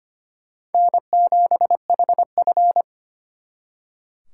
Morse code, heard as a keyed tone, about 33 wpm, 700 Hz. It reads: N7HF